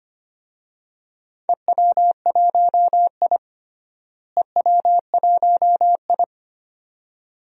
Morse code, heard as a keyed tone, about 25 wpm, 700 Hz. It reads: EW1I EW1I
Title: DXCC-100 Call Signs - 4 Characters - Repeated 2x - 25wpm